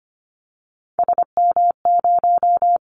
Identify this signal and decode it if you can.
Morse code, heard as a keyed tone, about 25 wpm, 700 Hz. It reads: SM0
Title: Callsign Prefixes for Top 25 DXCC - Repeated 3x - 25wpm